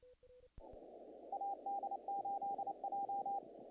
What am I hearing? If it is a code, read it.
Morse code, heard as a keyed tone, about 29 wpm, 800 Hz, with noise. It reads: AD8J
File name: contests / naqp-cw-k5tr-op-k5pi- -1800